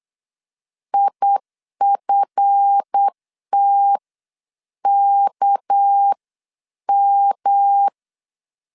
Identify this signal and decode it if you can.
Morse code, heard as a keyed tone, about 8 wpm, spaced as for 7 wpm, 800 Hz. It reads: IFTKM
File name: LEZIONI / 8 YJ45 / 8 YJ4 8wpm 3246